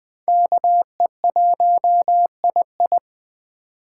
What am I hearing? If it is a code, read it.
Morse code, heard as a keyed tone, about 20 wpm, 700 Hz. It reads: KE1II
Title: USA Call Signs - 5 Characters - 3x Character Spacing - 20wpm